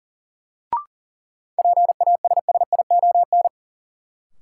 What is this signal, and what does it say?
Morse code, heard as a keyed tone, about 40 wpm, 700 Hz. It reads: PASSION
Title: Sentences from Top 1300 Words - New Word Drill Alt - 40wpm